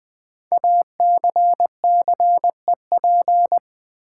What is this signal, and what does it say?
Morse code, heard as a keyed tone, about 20 wpm, 700 Hz. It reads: ACCEP
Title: Build A Word - 5 Letters At A Time - 10-Letter Words - 20wpm